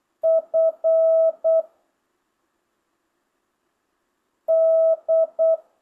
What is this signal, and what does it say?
Morse code, heard as a keyed tone, about 8 wpm, 650 Hz. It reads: F D
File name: jh8cbh / 2025cw